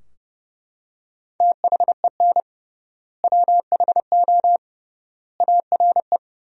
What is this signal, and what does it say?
Morse code, heard as a keyed tone, about 32 wpm, 700 Hz. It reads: THEN WHO ARE
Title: 3x Word-Spacing - Sentences from Top 100 Words - 30wpm